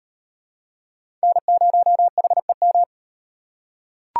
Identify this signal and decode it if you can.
Morse code, heard as a keyed tone, about 38 wpm, 700 Hz. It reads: N0HEM